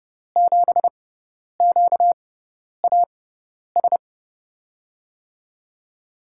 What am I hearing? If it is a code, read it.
Morse code, heard as a keyed tone, about 30 wpm, spaced as for 10 wpm, 700 Hz. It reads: 7QAS
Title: USA Call Signs - 6 Characters - 6x Character Spacing - 30wpm